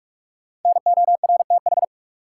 Morse code, heard as a keyed tone, about 46 wpm, 700 Hz. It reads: NORTH
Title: Sentences from Top 400 Words - New Word Drill - 45wpm